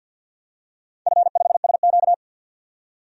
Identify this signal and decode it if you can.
Morse code, heard as a keyed tone, about 51 wpm, 700 Hz. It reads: F5SQ
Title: DXCC-100 Call Signs - 4 Characters - 50wpm